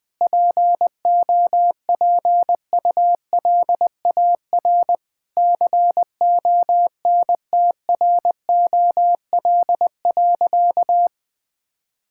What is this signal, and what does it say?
Morse code, heard as a keyed tone, about 21 wpm, 700 Hz. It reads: POPULAR CONTROL.
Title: Sentences from Top 1000 Words - Review Part 5 - 20wpm